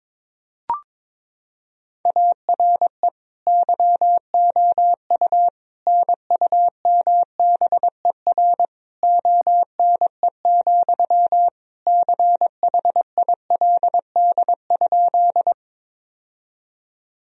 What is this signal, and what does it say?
Morse code, heard as a keyed tone, about 22 wpm, 700 Hz. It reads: ARE YOU NUMBER ONE, CHILD?
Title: Sentences from Top 1000 Words - Review Part 1 - 22wpm